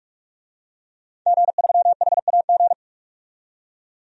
Morse code, heard as a keyed tone, about 45 wpm, 700 Hz. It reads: G3HAG
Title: DXCC-100 Call Signs - 5 Characters - 45wpm